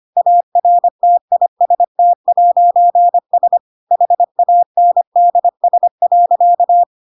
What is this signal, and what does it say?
Morse code, heard as a keyed tone, about 25 wpm, 700 Hz. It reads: ARTIST'S HANDS.
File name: Pinocchio / Pinocchio ch8 25 25 ik1hge